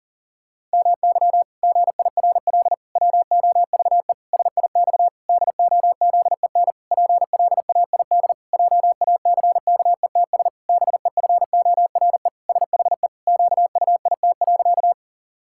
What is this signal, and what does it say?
Morse code, heard as a keyed tone, about 40 wpm, 700 Hz. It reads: MY GIRL WOVE SIX DOZEN PLAID JACKETS BEFORE SHE QUIT.
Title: Pangram Challenge v2 - 40wpm